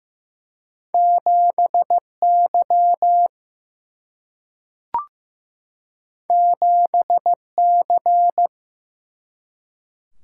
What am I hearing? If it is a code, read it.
Morse code, heard as a keyed tone, about 15 wpm, 700 Hz. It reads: 7Y 7C
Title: Single-Letter Single-Number 15wpm